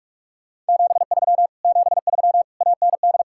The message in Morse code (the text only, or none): 73 73 AND